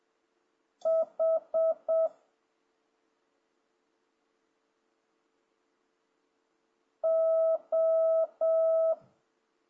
Morse code, heard as a keyed tone, about 7 wpm, 650 Hz. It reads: H O
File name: jh8cbh / 2026cw